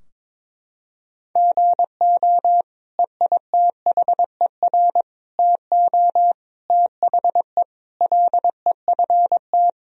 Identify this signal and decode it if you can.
Morse code, heard as a keyed tone, about 21 wpm, 700 Hz. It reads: GO EITHER TO THE LEFT
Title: Sentences from Top 1000 Words - Review Part 4 - 22wpm